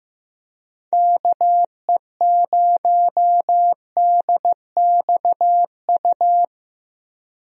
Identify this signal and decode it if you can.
Morse code, heard as a keyed tone, about 15 wpm, 700 Hz. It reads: KE0DXU
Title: USA Call Signs - 6 Characters - 15wpm